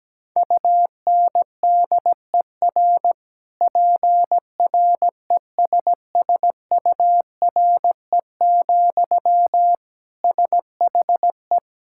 Morse code, heard as a keyed tone, about 17 wpm, 700 Hz. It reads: UNDER PRESSURE, SHE
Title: Sentences from Top 1000 Words - Review Part 6 - 17wpm